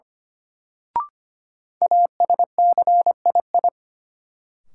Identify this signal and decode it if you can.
Morse code, heard as a keyed tone, about 25 wpm, 700 Hz. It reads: ASCII
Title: Sets of 5 Words - New Word Drill - 25wpm